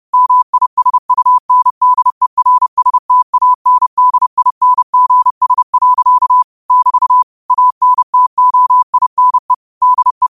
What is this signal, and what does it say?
Morse code, heard as a keyed tone, about 30 wpm, 1000 Hz. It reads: MISUNDERSTANDINGS. <BT> ANTOINE DE